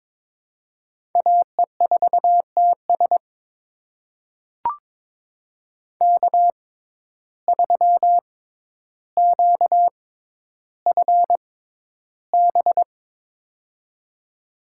Morse code, heard as a keyed tone, about 22 wpm, spaced as for 11 wpm, 700 Hz. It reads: AE4TS K3QFB
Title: USA Call Signs - 5 Characters - 6x Character Spacing - 22wpm